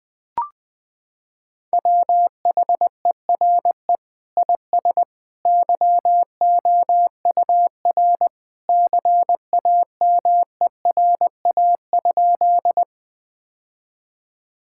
Morse code, heard as a keyed tone, about 21 wpm, 700 Hz. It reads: WHERE IS YOUR CAMERA?
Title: Sentences from Top 1000 Words - Review Part 5 - 20wpm